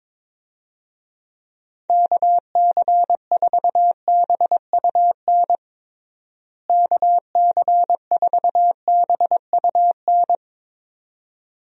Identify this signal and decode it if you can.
Morse code, heard as a keyed tone, about 22 wpm, 700 Hz. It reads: KC4BUN KC4BUN
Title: USA Call Signs - 6 Characters - Repeated 2x - 22wpm